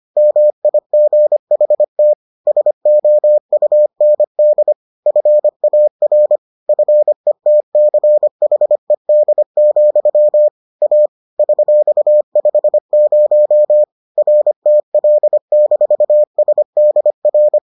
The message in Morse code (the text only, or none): MIGHT SOUND FAR FETCHED, A $50 RTL-SDR